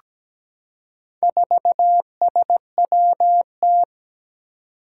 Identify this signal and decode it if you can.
Morse code, heard as a keyed tone, about 17 wpm, 700 Hz. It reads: 4SWT
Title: Four Challenging Letter-Number Combinations 17wpm